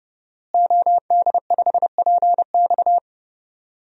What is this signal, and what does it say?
Morse code, heard as a keyed tone, about 30 wpm, 700 Hz. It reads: OD5PX
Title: DXCC-100 Call Signs - 5 Characters - Repeated 3x - 30wpm